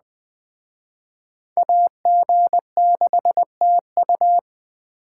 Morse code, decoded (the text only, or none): AG6TU